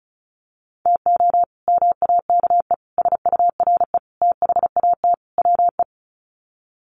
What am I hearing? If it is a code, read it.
Morse code, heard as a keyed tone, about 34 wpm, 700 Hz. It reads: TO MAKE SURE THAT WE
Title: Sets of 5 Words Encore 35wpm